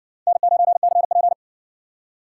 Morse code, heard as a keyed tone, about 60 wpm, 700 Hz. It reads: N9CP